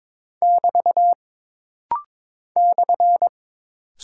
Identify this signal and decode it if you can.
Morse code, heard as a keyed tone, about 22 wpm, 700 Hz. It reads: <BT> /
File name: practice / 22wpm / Single Punctuation-Prosign - Rapid-Fire - 22wpm